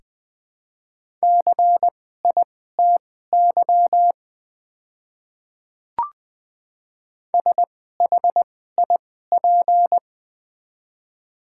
Morse code, heard as a keyed tone, about 21 wpm, spaced as for 15 wpm, 700 Hz. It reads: CITY SHIP
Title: 2x Character Spacing - Top 500 Words - 20wpm